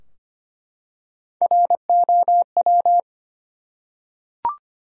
Morse code, heard as a keyed tone, about 25 wpm, 700 Hz. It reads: ROW
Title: Three Letter Words - 25wpm